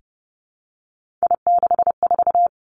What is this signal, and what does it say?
Morse code, heard as a keyed tone, about 31 wpm, 700 Hz. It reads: I64
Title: Three Challenging Letter-Number Combinations 30wpm